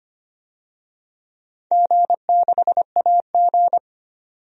Morse code, heard as a keyed tone, about 25 wpm, 700 Hz. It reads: G6AG
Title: DXCC-100 Call Signs - 4 Characters - 25wpm